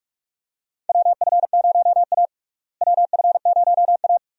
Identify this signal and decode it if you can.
Morse code, heard as a keyed tone, about 46 wpm, 700 Hz. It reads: WF0A WF0A
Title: USA Call Signs - 4 Characters - Repeated 2x - 45wpm